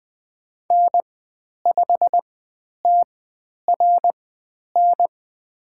Morse code, heard as a keyed tone, about 21 wpm, spaced as for 10 wpm, 700 Hz. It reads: N5TRN
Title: 102 Lesson 17g - Letter u - Call Signs 20wpm